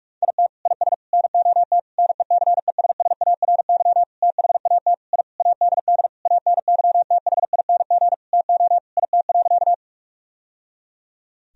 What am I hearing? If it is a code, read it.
Morse code, heard as a keyed tone, about 46 wpm, 700 Hz. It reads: IT IS NOT NECESSARY THAT I ADD ANYTHING TO IT.
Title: Sentences from Top 1000 Words - Review Part 1 - 45wpm